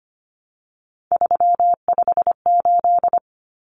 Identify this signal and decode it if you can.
Morse code, heard as a keyed tone, about 25 wpm, 700 Hz. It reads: 358
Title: Common Cut Numbers - 3-Digit 25wpm